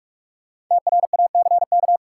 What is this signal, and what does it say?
Morse code, heard as a keyed tone, about 46 wpm, 700 Hz. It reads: TRACK
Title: Sentences from Top 1100 Words - New Word Drill - 45wpm